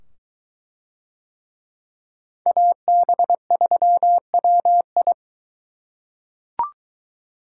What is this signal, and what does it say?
Morse code, heard as a keyed tone, about 23 wpm, 700 Hz. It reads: AB3WI